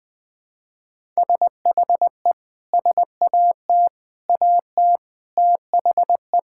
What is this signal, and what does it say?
Morse code, heard as a keyed tone, about 21 wpm, 700 Hz. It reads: SHE SAT AT THE
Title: Sentences with 6-Word Phrases - 20wpm